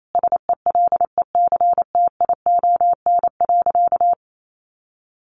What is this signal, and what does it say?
Morse code, heard as a keyed tone, about 28 wpm, 700 Hz. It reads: SELECTION.
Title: Sentences from Top 1400 Words 28wpm